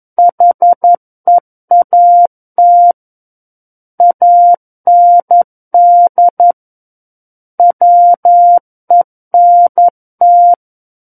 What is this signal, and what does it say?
Morse code, heard as a keyed tone, about 11 wpm, 700 Hz. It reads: HEAT AND WENT
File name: 11 WPM